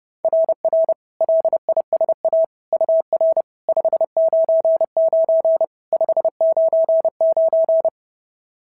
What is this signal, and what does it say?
Morse code, heard as a keyed tone, about 26 wpm, 650 Hz. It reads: RR LISA UR 599 599
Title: SOTA QSO Practice 30wpm